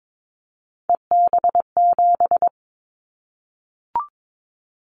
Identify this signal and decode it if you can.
Morse code, heard as a keyed tone, about 22 wpm, 700 Hz. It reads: EB7
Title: Callsign Prefixes for Top 25 DXCC - 22wpm